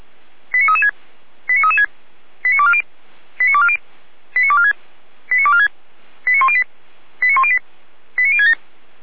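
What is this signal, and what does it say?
Five-tone selective call (ZVEI), heard as quick runs of five tones, tones of about 70 ms. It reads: 89336 89336 89240 89240 89246 89246 89108 89108 89067
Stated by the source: ZVEI five-tone selective call 89075, received by SDR.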